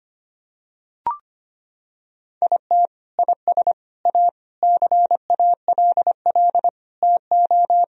Morse code, heard as a keyed tone, about 25 wpm, 700 Hz. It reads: IT IS A CALL TO
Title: Sentences from Top 1000 Words - Review Part 4 - 25wpm